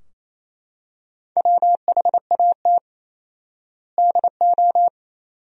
Morse code, heard as a keyed tone, about 28 wpm, 700 Hz. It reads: WHAT DO
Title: 4x Word-Spacing - Sentences from Top 200 Words - 28wpm